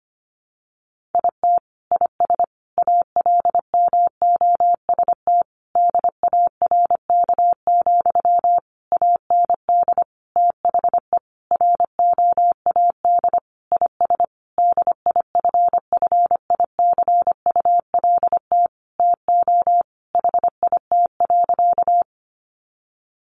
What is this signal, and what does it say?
Morse code, heard as a keyed tone, about 25 wpm, 700 Hz. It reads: IT IS ALMOST DARK, AND THE ROAD IS DIFFICULT TO HIT.
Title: Sentences from Top 1200 Words 25wpm